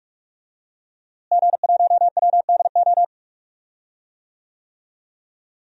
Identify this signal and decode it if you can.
Morse code, heard as a keyed tone, about 46 wpm, 700 Hz. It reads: G1WDO
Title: DXCC-100 Call Signs - 5 Characters - Repeated 2x - 45wpm